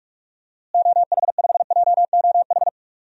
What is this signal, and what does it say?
Morse code, heard as a keyed tone, about 46 wpm, 700 Hz. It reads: OH5JOH
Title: DXCC-100 Call Signs - 6 Characters - Repeated 2x - 45wpm